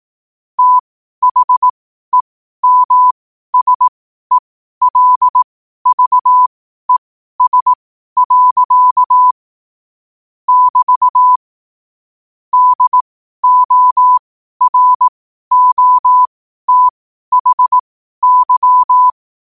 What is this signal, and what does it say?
Morse code, heard as a keyed tone, about 18 wpm, spaced as for 12 wpm, 1000 Hz. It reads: THEMSELVES. <BT> DOROTHY
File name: qotdfiles / 12wpm